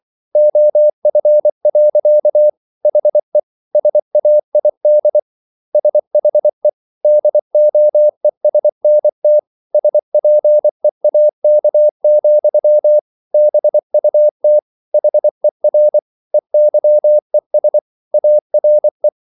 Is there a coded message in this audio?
Yes: Morse code from 0.4 to 19.1 s, heard as a keyed tone, about 24 wpm, 600 Hz. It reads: OF. HE SAID SHE DOESNT SPEAK, BUT HER EYES ARE